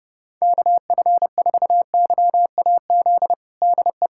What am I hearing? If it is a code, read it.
Morse code, heard as a keyed tone, about 30 wpm, 700 Hz. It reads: KF4YAZ DE
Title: USA Callsign DE Callsign 30wpm